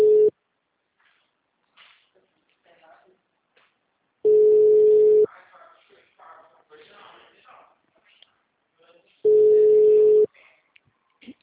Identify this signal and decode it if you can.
Ringback tone (Europe, 425 Hz), with speech.